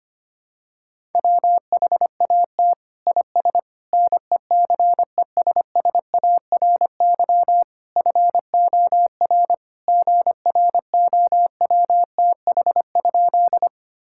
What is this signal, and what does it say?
Morse code, heard as a keyed tone, about 25 wpm, 700 Hz. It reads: WHAT IS NECESSARY FOR GROWTH?
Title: Sentences from Top 1000 Words - Review Part 5 - 25wpm